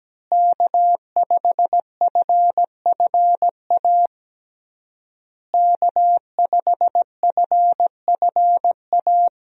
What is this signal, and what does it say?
Morse code, heard as a keyed tone, about 17 wpm, 700 Hz. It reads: K5FFA K5FFA
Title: USA Call Signs - 5 Characters - Repeated 3x - 17wpm